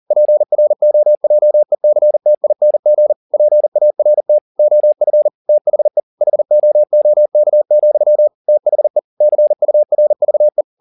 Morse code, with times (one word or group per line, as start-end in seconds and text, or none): 0.1-3.1 PROJECTING
3.3-4.4 PART
4.6-5.3 OF
5.5-6.0 THE
6.2-8.3 HOOK,
8.5-9.0 THE
9.2-10.6 CURVE